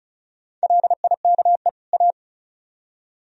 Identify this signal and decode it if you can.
Morse code, heard as a keyed tone, about 41 wpm, 700 Hz. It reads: LIKE A
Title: Speed-Racing - Sets of 2 Words 35wpm